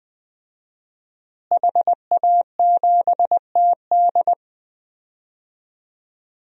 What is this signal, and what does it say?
Morse code, heard as a keyed tone, about 21 wpm, 700 Hz. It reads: HA7TD
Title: DXCC-100 Call Signs - 5 Characters - Repeated 3x - 20wpm